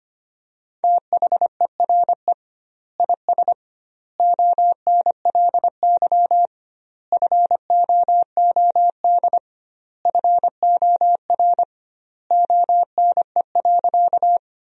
Morse code, heard as a keyed tone, about 25 wpm, 700 Hz. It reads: THERE IS ONLY FOOD FOR ONE.
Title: 2x Word-Spacing - Sentences from Top 300 Words - 25wpm